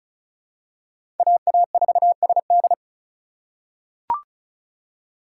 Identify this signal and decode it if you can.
Morse code, heard as a keyed tone, about 36 wpm, 700 Hz. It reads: AA4SD
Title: USA Call Signs - 5 Characters - 3x Character Spacing - 35wpm